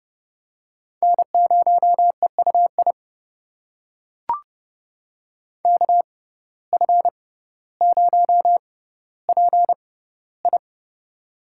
Morse code, heard as a keyed tone, about 30 wpm, spaced as for 15 wpm, 700 Hz. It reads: N0EUI KF0PI
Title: USA Call Signs - 5 Characters - 6x Character Spacing - 30wpm